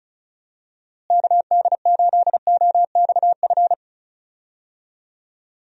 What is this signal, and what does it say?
Morse code, heard as a keyed tone, about 34 wpm, 700 Hz. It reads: KD8OXF
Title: USA Call Signs - 6 Characters - Repeated 3x - 35wpm